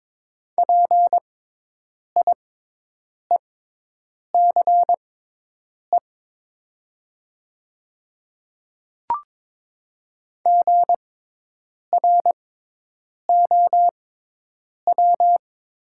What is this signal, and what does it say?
Morse code, heard as a keyed tone, about 22 wpm, spaced as for 8 wpm, 700 Hz. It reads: PIECE GROW
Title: 6x Character Spacing - Top 500 Words - 22wpm